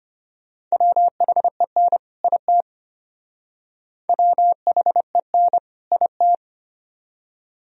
Morse code, heard as a keyed tone, about 32 wpm, 700 Hz. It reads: WHEN IT WHEN IT